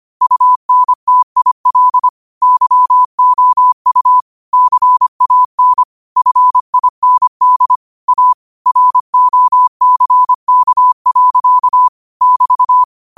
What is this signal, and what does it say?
Morse code, heard as a keyed tone, about 25 wpm, 1000 Hz. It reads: UNTIL YOU CAN FIND A ROCK. <BT>
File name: qotdfiles / 25wpm